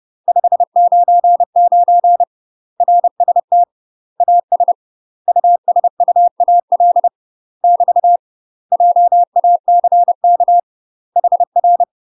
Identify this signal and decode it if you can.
Morse code, heard as a keyed tone, about 30 wpm, 700 Hz. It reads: 599 RST AS USUAL <BT> JACK HR
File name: UK MASTER QSOs TRAINING / 30WPM / UK MASTER QSOs 0034